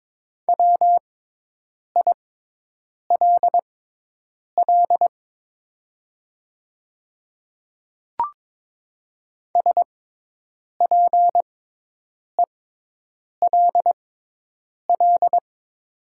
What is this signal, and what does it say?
Morse code, heard as a keyed tone, about 22 wpm, spaced as for 8 wpm, 700 Hz. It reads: WILL SPELL
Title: 6x Character Spacing - Top 200 Words - 22wpm